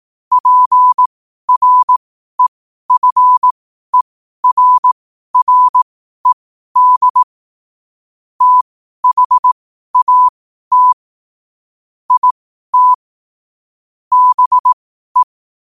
Morse code, heard as a keyed tone, about 18 wpm, spaced as for 12 wpm, 1000 Hz. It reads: PREFERRED THAT IT BE
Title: 12 WPM morse code quotes for Fri, 01 Mar 2024